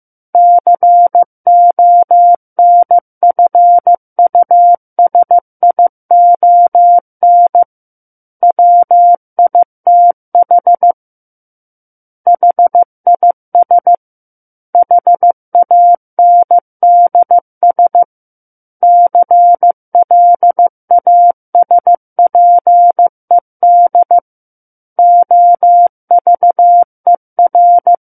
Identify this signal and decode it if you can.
Morse code, heard as a keyed tone, about 15 wpm, 700 Hz. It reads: CONFUSION WITH HIS HANDS CLASPED OVER
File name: Chapter 6 - 15 WPM